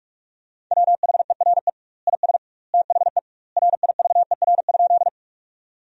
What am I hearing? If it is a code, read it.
Morse code, heard as a keyed tone, about 43 wpm, 700 Hz. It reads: WHERE IS THE RIVER?
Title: 2x Word-Spacing - Sentences from Top 300 Words - 45wpm